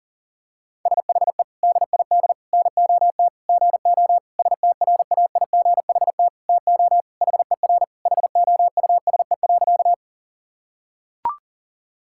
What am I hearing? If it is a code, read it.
Morse code, heard as a keyed tone, about 40 wpm, 700 Hz. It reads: SHE DID NOT GO STRAIGHT TO HER HOUSE.
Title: Sentences from Top 1000 Words - Review Part 6 - 40wpm